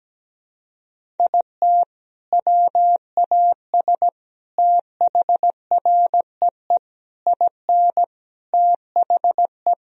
Morse code, heard as a keyed tone, about 17 wpm, 700 Hz. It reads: IT WAS THREE IN THE
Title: Sentences from Top 1000 Words - Review Part 1 - 17wpm